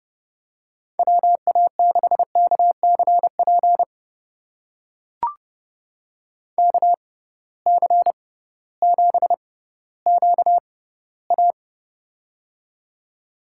Morse code, heard as a keyed tone, about 32 wpm, spaced as for 17 wpm, 700 Hz. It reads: WA6KCP KC7QA
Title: USA Call Signs - 6x Character Spacing - 30wpm